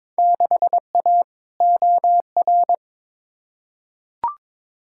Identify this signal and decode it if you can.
Morse code, heard as a keyed tone, about 21 wpm, 700 Hz. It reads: 6A OR
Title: Field Day Exchanges - 22wpm